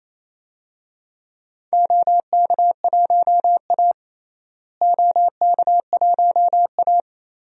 Morse code, heard as a keyed tone, about 27 wpm, 700 Hz. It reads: OK1A OK1A
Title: DXCC-100 Call Signs - 4 Characters - Repeated 2x - 28wpm